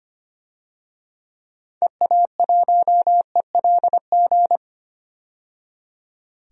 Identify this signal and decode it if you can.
Morse code, heard as a keyed tone, about 25 wpm, 700 Hz. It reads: EA1ELG
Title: DXCC-100 Call Signs - 6 Characters - 25wpm